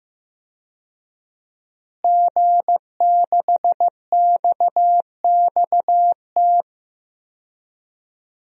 Morse code, heard as a keyed tone, about 15 wpm, 700 Hz. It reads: G6XXT
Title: DXCC-100 Call Signs - 5 Characters - Repeated 2x - 15wpm